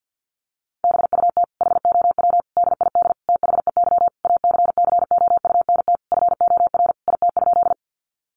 Morse code, heard as a keyed tone, about 50 wpm, 700 Hz. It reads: BUT HOW DID THEY ACCOUNT FOR IT?
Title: Sentences from Top 900 Words 50wpm